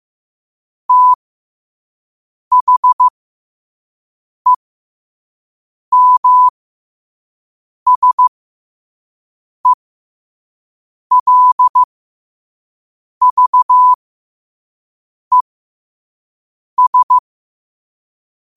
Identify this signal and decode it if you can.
Morse code, heard as a keyed tone, about 15 wpm, spaced as for 5 wpm, 1000 Hz. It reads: THEMSELVES
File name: qotdfiles / 5wpm